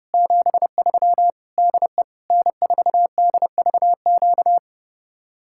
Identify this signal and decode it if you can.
Morse code, heard as a keyed tone, about 30 wpm, 700 Hz. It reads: 73 DE N4DVQ